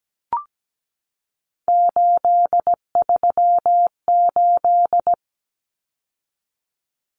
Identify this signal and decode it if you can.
Morse code, heard as a keyed tone, about 17 wpm, 700 Hz. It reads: 838
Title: Numbers - 3-Digit 17wpm